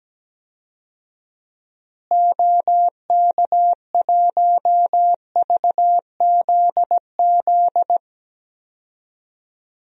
Morse code, heard as a keyed tone, about 17 wpm, 700 Hz. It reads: OK1VZZ
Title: DXCC-100 Call Signs - 6 Characters - Repeated 2x - 17wpm